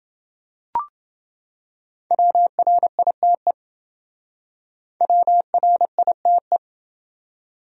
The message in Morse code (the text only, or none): WRITE WRITE